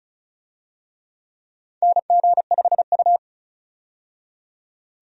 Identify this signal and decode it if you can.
Morse code, heard as a keyed tone, about 36 wpm, 700 Hz. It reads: NG5U
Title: USA Call Signs - 4 Characters - 35wpm